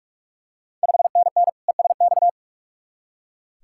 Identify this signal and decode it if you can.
Morse code, heard as a keyed tone, about 44 wpm, 700 Hz. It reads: 5NN ESX
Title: Ontario QSO Party Exchange 45wpm